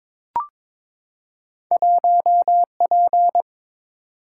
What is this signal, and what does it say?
Morse code, heard as a keyed tone, about 22 wpm, 690 Hz. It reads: 1P
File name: practice / 22wpm / Two Letter-Number Combinations - 22wpm